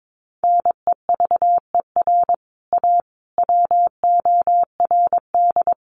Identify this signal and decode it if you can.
Morse code, heard as a keyed tone, about 22 wpm, 700 Hz. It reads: NEVER A WORD
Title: Sentences from Top 1300 Words 22wpm